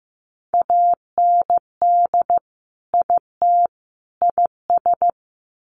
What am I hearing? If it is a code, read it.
Morse code, heard as a keyed tone, about 15 wpm, 700 Hz. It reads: AND IT IS
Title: Sets of 3 Words 15wpm